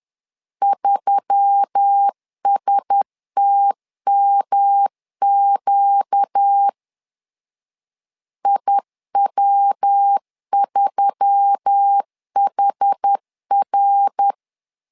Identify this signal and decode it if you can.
Morse code, heard as a keyed tone, about 10 wpm, 800 Hz. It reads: 3STMQ IW3HR
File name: LEZIONI / 4 Q1S3 10wpm 2427